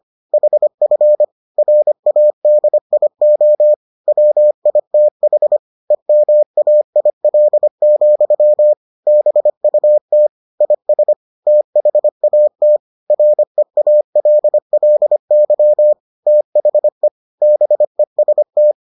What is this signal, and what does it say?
Morse code, heard as a keyed tone, about 25 wpm, 600 Hz. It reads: HF RADIO WITH EMAIL, BUT IS THAT REALLY THE BEST